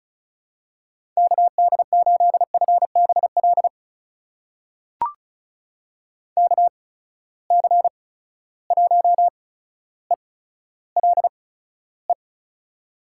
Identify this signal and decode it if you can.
Morse code, heard as a keyed tone, about 36 wpm, spaced as for 15 wpm, 700 Hz. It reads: KD8FBL KC1ELE
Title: USA Call Signs - 6 Characters - 8x Character Spacing - 35wpm